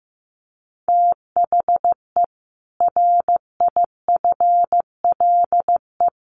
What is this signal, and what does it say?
Morse code, heard as a keyed tone, about 15 wpm, 700 Hz. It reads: THE RIFLE
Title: A Sound Of Thunder 15wpm